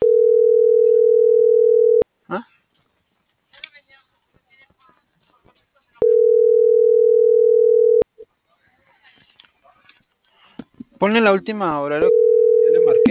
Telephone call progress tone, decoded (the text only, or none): ringback tone